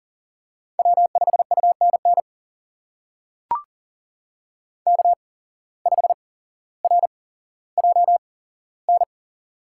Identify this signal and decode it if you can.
Morse code, heard as a keyed tone, about 40 wpm, spaced as for 17 wpm, 700 Hz. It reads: W5UNN K5RJN
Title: USA Call Signs - 5 Characters - 8x Character Spacing - 40wpm